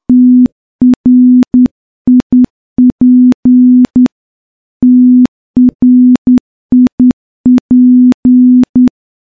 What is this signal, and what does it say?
Morse code, heard as a keyed tone, about 10 wpm, 250 Hz. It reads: TRIP TRIP
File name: morse-training-10WPM